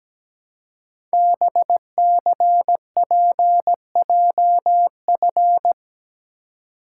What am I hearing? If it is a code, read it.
Morse code, heard as a keyed tone, about 17 wpm, 700 Hz. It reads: BCPJF
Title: Code Groups - 5 Letters - 17wpm